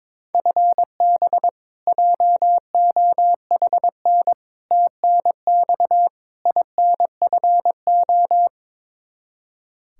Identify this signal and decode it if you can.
Morse code, heard as a keyed tone, about 22 wpm, 700 Hz. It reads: FB JOHN TNX INFO
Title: Practice QSO - Exchange 3 - More Info - 22wpm